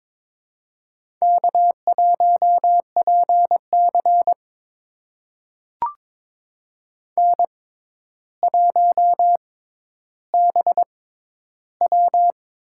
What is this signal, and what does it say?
Morse code, heard as a keyed tone, about 22 wpm, spaced as for 11 wpm, 700 Hz. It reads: K1PC N1BW
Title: USA Call Signs - 4 Characters - 6x Character Spacing - 22wpm